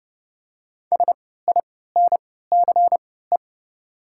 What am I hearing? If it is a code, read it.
Morse code, heard as a keyed tone, about 30 wpm, spaced as for 17 wpm, 700 Hz. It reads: SINCE
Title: 3x Character Spacing - Top 500 Words - 30wpm